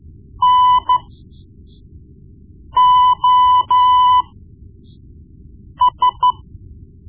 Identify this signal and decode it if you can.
Morse code, heard as a keyed tone, about 14 wpm, spaced as for 4 wpm, 1000 Hz. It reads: NOS
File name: Morse Lessons / Morse Lessons / Tape-1B